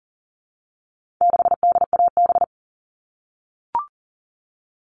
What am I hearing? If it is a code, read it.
Morse code, heard as a keyed tone, about 40 wpm, 700 Hz. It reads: 6DAB
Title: All Cut Numbers - 4-Digit 40wpm